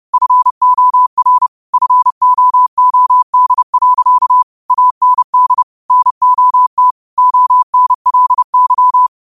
Morse code, heard as a keyed tone, about 30 wpm, 1000 Hz. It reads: FOR FOOD. AND NOT ONLY
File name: qotdfiles / 30wpm